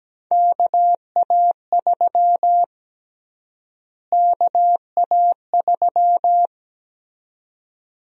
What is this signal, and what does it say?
Morse code, heard as a keyed tone, about 17 wpm, 700 Hz. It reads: KA3 KA3
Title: USA Call Sign Prefixes - Repeated 2x - 17wpm